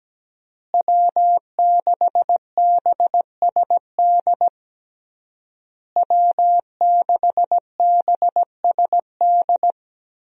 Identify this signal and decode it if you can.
Morse code, heard as a keyed tone, about 17 wpm, 700 Hz. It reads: W6BSD W6BSD
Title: USA Call Signs - 5 Characters - Repeated 3x - 17wpm